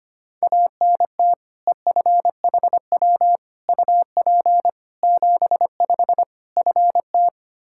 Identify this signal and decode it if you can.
Morse code, heard as a keyed tone, about 25 wpm, 700 Hz. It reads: ANT EFHW UP 75 FT
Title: K1USN SST QSO Practice 25wpm